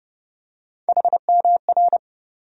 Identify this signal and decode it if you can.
Morse code, heard as a keyed tone, about 30 wpm, 700 Hz. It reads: HMR
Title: USA Call Sign Postfixes - 3 Letters 30wpm